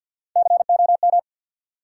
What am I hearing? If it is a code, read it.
Morse code, heard as a keyed tone, about 51 wpm, 700 Hz. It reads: COM
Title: Sentences from Top 1300 Words - New Word Drill Alt - 50wpm